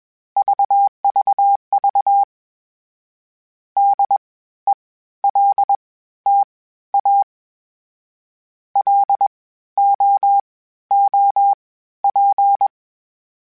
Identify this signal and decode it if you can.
Morse code, heard as a keyed tone, about 21 wpm, spaced as for 13 wpm, 800 Hz. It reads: VVV DELTA LOOP
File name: (spaced) / FromSMs spaced mp3s / Antennas sp@25